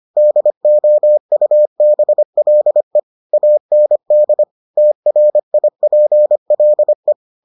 Morse code, heard as a keyed tone, about 25 wpm, 600 Hz. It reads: DOUBLE AND TRIPLE